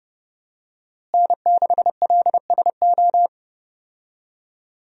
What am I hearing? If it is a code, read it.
Morse code, heard as a keyed tone, about 31 wpm, 700 Hz. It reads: N6LSO
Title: USA Call Signs - 5 Characters - Repeated 2x - 30wpm